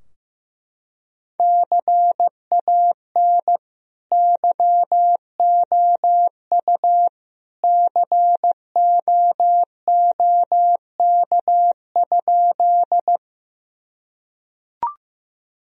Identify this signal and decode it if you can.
Morse code, heard as a keyed tone, about 15 wpm, 700 Hz. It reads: CAN YOU COOK?
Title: Sentences from Top 1000 Words - Review Part 4 - 15wpm